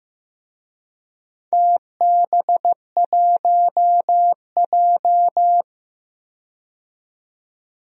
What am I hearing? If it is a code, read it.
Morse code, heard as a keyed tone, about 15 wpm, 700 Hz. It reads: TB1J
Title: DXCC-100 Call Signs - 4 Characters - 15wpm